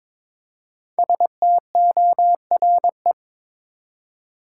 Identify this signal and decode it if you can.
Morse code, heard as a keyed tone, about 22 wpm, 700 Hz. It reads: STORE
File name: 5 Letter Words - First 3 Letters Given - 22wpm